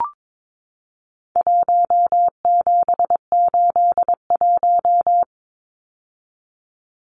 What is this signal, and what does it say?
Morse code, heard as a keyed tone, about 22 wpm, 700 Hz. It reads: 1781
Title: Numbers - Contest Style - 4-Digit 22wpm